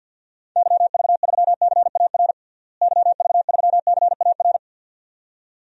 Morse code, heard as a keyed tone, about 53 wpm, 700 Hz. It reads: YV3CAR YV3CAR
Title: DXCC-100 Call Signs - 6 Characters - Repeated 2x - 50wpm